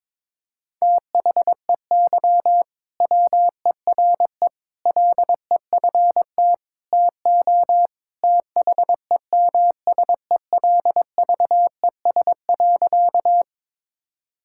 Morse code, heard as a keyed tone, about 22 wpm, 700 Hz. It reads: THEY WERE LEFT TO THEMSELVES.
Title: Sentences from Top 1000 Words - Review Part 5 - 22wpm